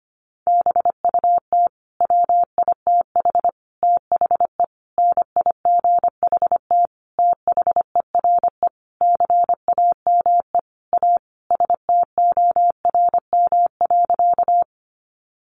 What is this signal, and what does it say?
Morse code, heard as a keyed tone, about 25 wpm, 700 Hz. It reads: BUT WITH THE NIGHT THERE CAME A STORM.
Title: Sentences from Top 1500 Words 25wpm